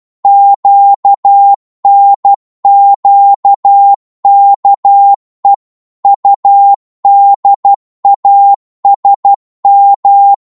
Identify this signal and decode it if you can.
Morse code, heard as a keyed tone, about 12 wpm, 800 Hz. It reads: QNQKE UDASM
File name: Morse / ltr-12wpm